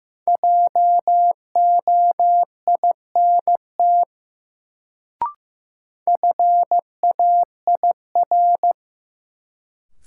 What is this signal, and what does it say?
Morse code, heard as a keyed tone, about 15 wpm, 700 Hz. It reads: JOINT FAIR
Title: Sentences from Top 1300 Words - New Word Drill Alt - 15wpm